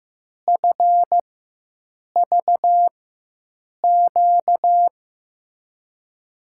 Morse code, heard as a keyed tone, about 15 wpm, spaced as for 7 wpm, 700 Hz. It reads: FVQ
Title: USA Call Signs - 4x Character Spacing - 15wpm